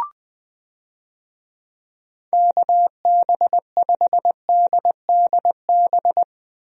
Morse code, heard as a keyed tone, about 20 wpm, 700 Hz. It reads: KB5DDB
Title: USA Call Signs - 6 Characters - Repeated 2x - 20wpm